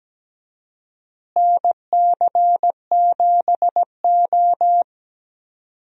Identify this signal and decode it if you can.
Morse code, heard as a keyed tone, about 17 wpm, 700 Hz. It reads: NC7O
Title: USA Call Signs - 4 Characters - Repeated 3x - 17wpm